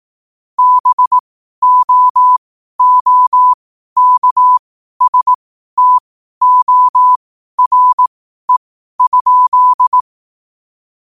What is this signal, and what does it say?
Morse code, heard as a keyed tone, about 18 wpm, spaced as for 12 wpm, 1000 Hz. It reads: BOOKSTORE?
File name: qotdfiles / 12wpm